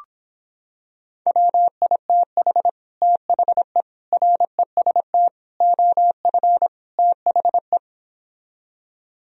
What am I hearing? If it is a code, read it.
Morse code, heard as a keyed tone, about 26 wpm, 700 Hz. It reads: WITH THE REST OF THE